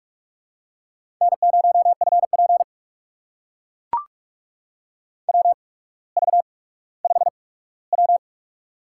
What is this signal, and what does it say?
Morse code, heard as a keyed tone, about 44 wpm, spaced as for 19 wpm, 690 Hz. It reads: N0FP WV5W
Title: USA Call Signs - 4 Characters - 8x Character Spacing - 45wpm